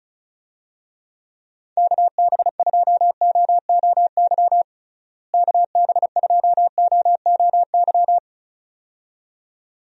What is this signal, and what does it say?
Morse code, heard as a keyed tone, about 36 wpm, 700 Hz. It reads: KB2OOY KB2OOY
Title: USA Call Signs - 6 Characters - Repeated 2x - 35wpm